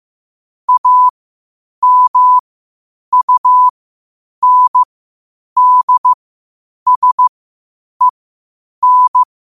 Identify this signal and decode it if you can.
Morse code, heard as a keyed tone, about 15 wpm, spaced as for 8 wpm, 1000 Hz. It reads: AMUNDSEN
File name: qotdfiles / 8wpm